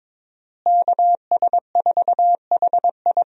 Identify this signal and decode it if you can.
Morse code, heard as a keyed tone, about 22 wpm, 700 Hz. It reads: KS4HI
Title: USA Call Signs - 5 Characters - Repeated 3x - 22wpm